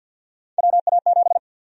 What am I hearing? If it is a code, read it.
Morse code, heard as a keyed tone, about 52 wpm, 700 Hz. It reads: WA7
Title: USA Call Sign Prefixes - Repeated 2x - 50wpm